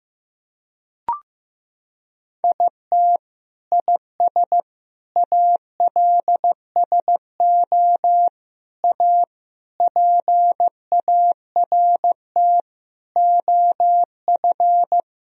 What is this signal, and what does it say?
Morse code, heard as a keyed tone, about 16 wpm, 700 Hz. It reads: IT IS ALSO A PART OF
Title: Sentences from Top 1000 Words - Review Part 1 - 15wpm